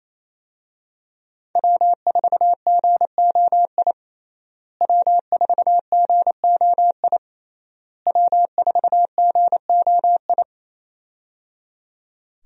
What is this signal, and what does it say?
Morse code, heard as a keyed tone, about 28 wpm, 700 Hz. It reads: W4GOI W4GOI W4GOI
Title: USA Call Signs - 5 Characters - Repeated 3x - 28wpm